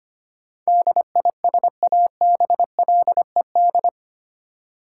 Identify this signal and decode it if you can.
Morse code, heard as a keyed tone, about 25 wpm, 690 Hz. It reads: DISABLED
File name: 8 Letter Words - First 3 Letters Given - 25wpm